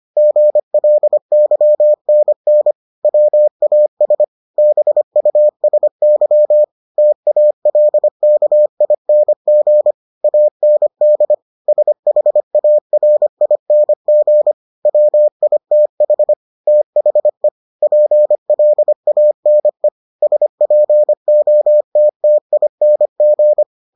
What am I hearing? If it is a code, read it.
Morse code, heard as a keyed tone, about 25 wpm, 600 Hz. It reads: GLYNN WAS BUSY TALKING AND SHARING WITH THE PLANE SPOTTING